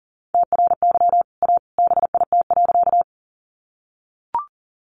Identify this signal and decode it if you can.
Morse code, heard as a keyed tone, about 40 wpm, 700 Hz. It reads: TRY A BIT.
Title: Sentences from Top 900 Words 40wpm